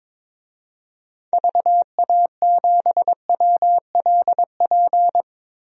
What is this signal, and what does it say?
Morse code, heard as a keyed tone, about 22 wpm, 700 Hz. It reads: VA7WLP
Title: DXCC-100 Call Signs - 6 Characters - Repeated 2x - 22wpm